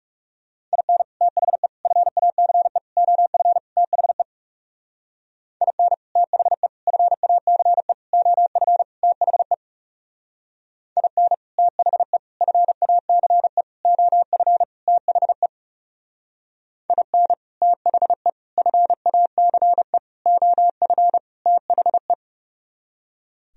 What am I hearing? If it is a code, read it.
Morse code, heard as a keyed tone, about 39 wpm, 700 Hz. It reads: IN THE FACE OF THE IN THE FACE OF THE IN THE FACE OF THE IN THE FACE OF THE